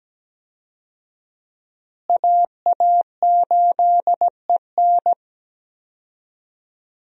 Morse code, heard as a keyed tone, about 17 wpm, 700 Hz. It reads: AA8EN
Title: USA Call Signs - 5 Characters - Repeated 2x - 17wpm